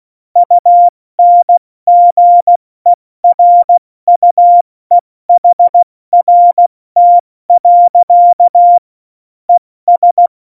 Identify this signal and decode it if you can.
Morse code, heard as a keyed tone, about 16 wpm, spaced as for 14 wpm, 700 Hz. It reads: UNGERUEHRT. ES
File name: Morsekurs / Witze / Witz 04 August 2020, 70 F80